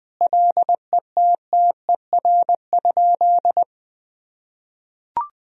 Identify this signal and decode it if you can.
Morse code, heard as a keyed tone, about 20 wpm, 700 Hz. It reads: LETTER?
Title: Sentences from Top 300 Words and Abbreviations 20wpm